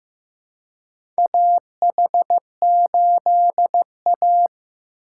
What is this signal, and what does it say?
Morse code, heard as a keyed tone, about 15 wpm, 700 Hz. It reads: AH8A
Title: USA Call Signs - 3x Character Spacing - 15wpm